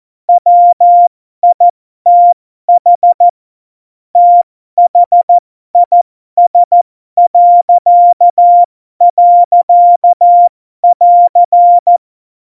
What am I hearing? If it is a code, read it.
Morse code, heard as a keyed tone, about 14 wpm, spaced as for 12 wpm, 700 Hz. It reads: WITH THIS..<AR>
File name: Morsekurs / Witze